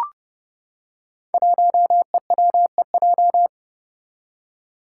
Morse code, heard as a keyed tone, about 31 wpm, 700 Hz. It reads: 1EWEJ
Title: Five Challenging Letter-Number Combinations 30wpm